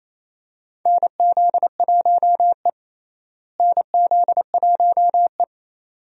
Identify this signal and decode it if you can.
Morse code, heard as a keyed tone, about 28 wpm, 700 Hz. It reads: NZ1E NZ1E
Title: USA Call Signs - 4 Characters - Repeated 2x - 28wpm